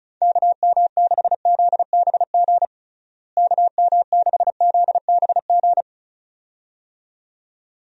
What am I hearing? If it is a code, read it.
Morse code, heard as a keyed tone, about 35 wpm, 700 Hz. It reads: KM6ZBG KM6ZBG
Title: USA Call Signs - 6 Characters - Repeated 2x - 35wpm